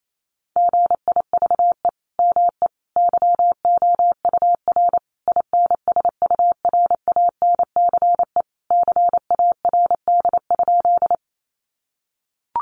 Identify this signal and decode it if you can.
Morse code, heard as a keyed tone, about 28 wpm, 700 Hz. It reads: GIVE ME YOUR INSURANCE CARD?
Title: Sentences from Top 1100 Words 28wpm